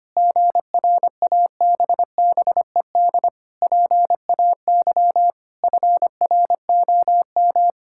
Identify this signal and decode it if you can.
Morse code, heard as a keyed tone, about 25 wpm, 700 Hz. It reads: GRABBED PAY FROM
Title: Pangram Challenge v2 - 25wpm